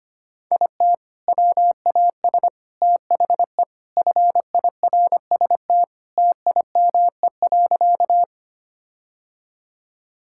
Morse code, heard as a keyed tone, about 25 wpm, 700 Hz. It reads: IT WAS THE FIRST TIME.
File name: practice / 25wpm / Sentences with 4-Word Phrases - 25wpm